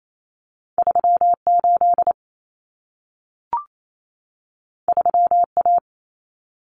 Morse code, heard as a keyed tone, about 28 wpm, 700 Hz. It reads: 38 3A
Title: Common Cut Numbers - 2-Digit 28wpm